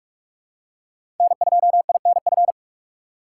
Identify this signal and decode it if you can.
Morse code, heard as a keyed tone, about 46 wpm, 700 Hz. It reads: N2INF
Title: USA Call Signs - 2x Character Spacing - 45wpm